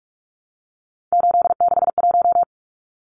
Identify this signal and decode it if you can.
Morse code, heard as a keyed tone, about 44 wpm, 700 Hz. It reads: 861
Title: Numbers - 3-Digit 45wpm